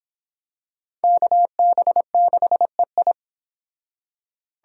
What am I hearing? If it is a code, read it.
Morse code, heard as a keyed tone, about 26 wpm, 700 Hz. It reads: KB6EI